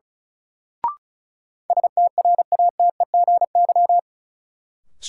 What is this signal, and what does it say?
Morse code, heard as a keyed tone, about 36 wpm, 700 Hz. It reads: STRATEGY
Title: Sentences from Top 1000 Words - Review Part 5 - New Word Drill - 35wpm